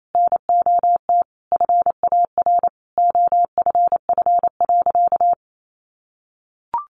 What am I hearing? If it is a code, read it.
Morse code, heard as a keyed tone, about 28 wpm, 700 Hz. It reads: NOT FAR OFF.
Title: Sentences from Top 1400 Words 28wpm